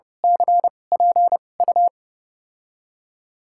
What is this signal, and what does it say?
Morse code, heard as a keyed tone, about 32 wpm, spaced as for 22 wpm, 700 Hz. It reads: CPU
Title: USA Call Signs - 2x Character Spacing - 30wpm